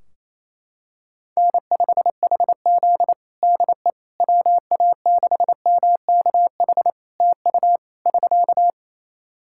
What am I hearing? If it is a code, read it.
Morse code, heard as a keyed tone, about 28 wpm, 700 Hz. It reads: N5HZ DE WA6MKH TU <SK>
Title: Practice QSO - Exchange 4 - Ending - 28wpm